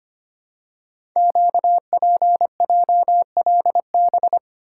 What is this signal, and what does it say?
Morse code, heard as a keyed tone, about 25 wpm, 700 Hz. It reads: QPJLB
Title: Code Groups - 5 Letters - 25wpm